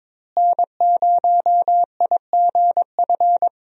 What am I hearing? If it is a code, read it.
Morse code, heard as a keyed tone, about 22 wpm, 700 Hz. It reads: N0IGF
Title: USA Call Signs - 5 Characters - Repeated 3x - 22wpm